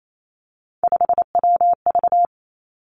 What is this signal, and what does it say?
Morse code, heard as a keyed tone, about 28 wpm, 700 Hz. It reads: 5WV
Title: Three Challenging Letter-Number Combinations 28wpm